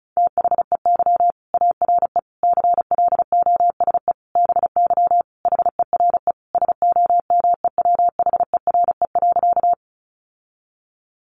Morse code, heard as a keyed tone, about 34 wpm, 700 Hz. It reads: THEY ARE CLOSE BY HERE SOMEWHERE.
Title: Sentences from Top 1400 Words 35wpm